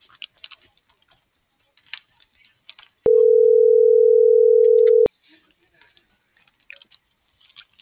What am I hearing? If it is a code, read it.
Ringback tone (North America), 440 + 480 Hz.